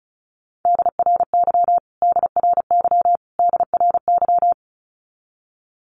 Morse code, heard as a keyed tone, about 34 wpm, 700 Hz. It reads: DRY DRY DRY
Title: Top 500 Words Repeated 35wpm